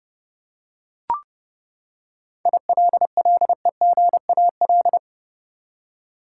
Morse code, heard as a keyed tone, about 32 wpm, 700 Hz. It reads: ILLEGAL
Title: Seven Letter Words - 30wpm